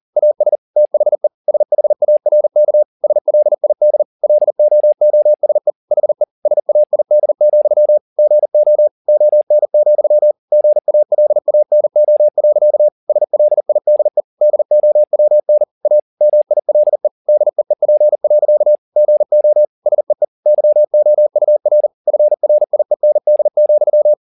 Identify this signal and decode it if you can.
Morse code, heard as a keyed tone, about 40 wpm, 600 Hz. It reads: AS THE SHARK SLID LOOSE HE SAID, GO ON, GALANO. SLIDE DOWN A MILE DEEP. GO SEE YOUR FRIEND,
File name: the old man and the sea 2 40wpm